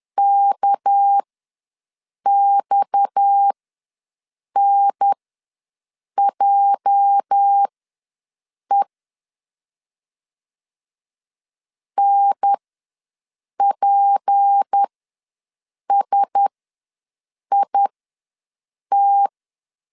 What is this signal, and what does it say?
Morse code, heard as a keyed tone, about 10 wpm, spaced as for 6 wpm, 800 Hz. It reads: KXNJE NPSIT
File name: LEZIONI / 8 YJ45 / 8 YJ45 ws 10wpm 3843